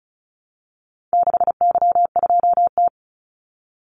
Morse code, heard as a keyed tone, about 34 wpm, 700 Hz. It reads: 6Y2T
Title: Call Signs Repeated 3x - 35wpm